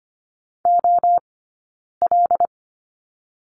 Morse code, heard as a keyed tone, about 25 wpm, 700 Hz. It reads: O L
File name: course / 25:10wpm / 092 Lesson 16d - Letter c - Two Characters 25wpm